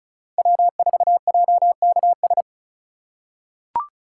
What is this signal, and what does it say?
Morse code, heard as a keyed tone, about 36 wpm, 700 Hz. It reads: W4JKS
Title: USA Call Signs - 5 Characters - 3x Character Spacing - 35wpm